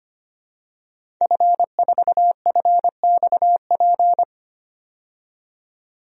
Morse code, heard as a keyed tone, about 25 wpm, 700 Hz. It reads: F4FXP
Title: DXCC-100 Call Signs - 5 Characters - 25wpm